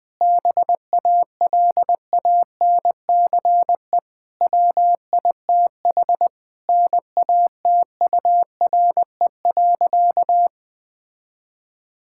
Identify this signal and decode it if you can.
Morse code, heard as a keyed tone, about 20 wpm, 700 Hz. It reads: BALANCE WITH NATURE.
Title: Sentences from Top 1000 Words - Review Part 6 - 20wpm